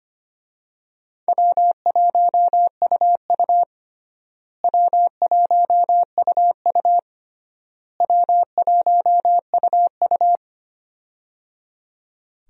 Morse code, heard as a keyed tone, about 25 wpm, 700 Hz. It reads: W1UU W1UU W1UU
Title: USA Call Signs - 4 Characters - Repeated 3x - 25wpm